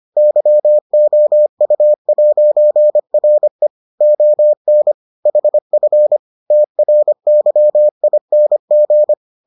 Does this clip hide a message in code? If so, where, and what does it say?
Yes: Morse code from 0.2 to 9.1 s, heard as a keyed tone, about 25 wpm, 600 Hz. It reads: YOU'RE ON HF TRYING